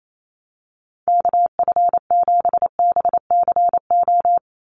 Morse code, heard as a keyed tone, about 28 wpm, 700 Hz. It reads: KF7BCO
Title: USA Call Signs 28wpm